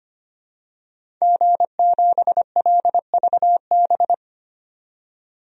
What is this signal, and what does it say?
Morse code, heard as a keyed tone, about 25 wpm, 700 Hz. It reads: G7LVB
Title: DXCC-100 Call Signs - 5 Characters - 25wpm